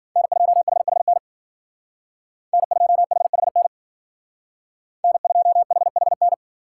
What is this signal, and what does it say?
Morse code, heard as a keyed tone, about 48 wpm, 700 Hz. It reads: N2HHN N2HHN N2HHN
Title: Speed Racing - USA 1x3 Call Signs 40wpm